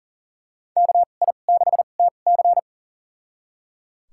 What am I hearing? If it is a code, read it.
Morse code, heard as a keyed tone, about 37 wpm, spaced as for 28 wpm, 700 Hz. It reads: KI6TC
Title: USA Call Signs - 5 Characters - 2x Character Spacing - 40wpm